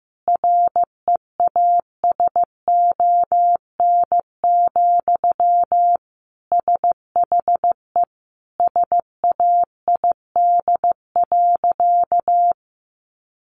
Morse code, heard as a keyed tone, about 15 wpm, 700 Hz. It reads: REASON, SHE SAID.
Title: Sentences from Top 700 Words 15wpm